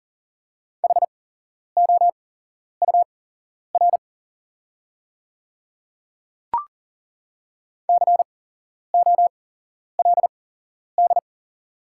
Morse code, heard as a keyed tone, about 40 wpm, spaced as for 11 wpm, 700 Hz. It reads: HOUR COLD
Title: 8x Character Spacing - Top 500 Words - 40wpm